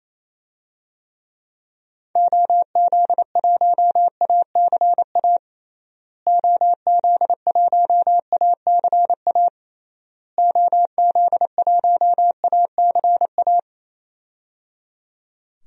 Morse code, heard as a keyed tone, about 28 wpm, 700 Hz. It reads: OZ1ACA OZ1ACA OZ1ACA
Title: DXCC-100 Call Signs - 6 Characters - Repeated 3x - 28wpm